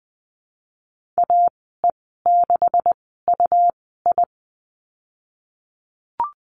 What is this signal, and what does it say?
Morse code, heard as a keyed tone, about 20 wpm, spaced as for 15 wpm, 700 Hz. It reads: AE6UI
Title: 140 Lesson 23g - Letter f - Call Signs 20wpm